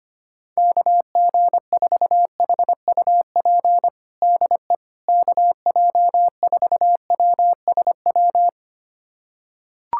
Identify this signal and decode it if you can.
Morse code, heard as a keyed tone, about 25 wpm, 700 Hz. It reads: KG4HUP DE KJ4WSW
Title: USA Callsign DE Callsign - 6 Characters - 25wpm